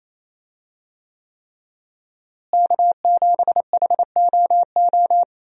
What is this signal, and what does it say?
Morse code, heard as a keyed tone, about 27 wpm, 700 Hz. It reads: K7HOO
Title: USA Call Signs - 5 Characters - Repeated 3x - 28wpm